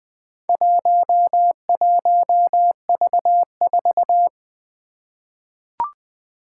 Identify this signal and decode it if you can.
Morse code, heard as a keyed tone, about 20 wpm, 700 Hz. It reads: 11V4
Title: Four Challenging Letter-Number Combinations 20wpm